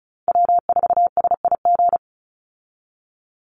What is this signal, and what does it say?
Morse code, heard as a keyed tone, about 34 wpm, 700 Hz. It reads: W4SIG
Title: Call Signs Repeated 3x - 35wpm